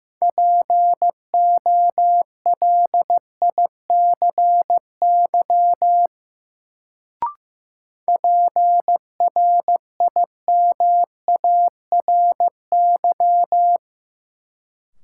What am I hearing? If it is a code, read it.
Morse code, heard as a keyed tone, about 15 wpm, 700 Hz. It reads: POLICY PRIMARY
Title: Sentences from Top 1000 Words - Review Part 5 - New Word Drill - 15wpm